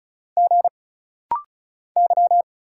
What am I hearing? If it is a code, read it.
Morse code, heard as a keyed tone, about 34 wpm, 700 Hz. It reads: G Y
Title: Single Letter-Number Alt - Rapid-Fire 35wpm